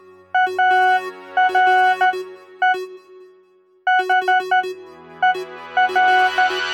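Morse code, heard as a keyed tone, about 12 wpm, spaced as for 10 wpm, 740 Hz. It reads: ARE HER
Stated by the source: Morse code at the beginning of an experimental music track.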